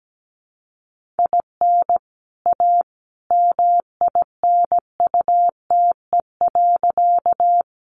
Morse code, heard as a keyed tone, about 17 wpm, 700 Hz. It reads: IN A MINUTE.
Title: Sentences from Top 500 Words 17wpm